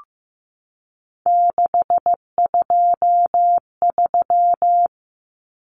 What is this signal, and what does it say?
Morse code, heard as a keyed tone, about 15 wpm, 700 Hz. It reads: 623
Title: Common Cut Numbers - 3-Digit 15wpm